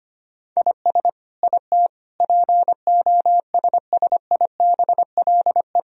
Morse code, heard as a keyed tone, about 25 wpm, 700 Hz. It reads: IS IT POSSIBLE
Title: Sentences from Top 1000 Words - Review Part 3 - 25wpm